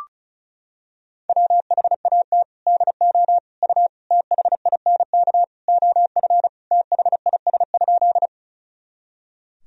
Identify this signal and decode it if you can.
Morse code, heard as a keyed tone, about 36 wpm, 700 Hz. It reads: WHAT DO U THINK OF THIS?
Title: Sentences from Top 200 Words and Abbreviations 35wpm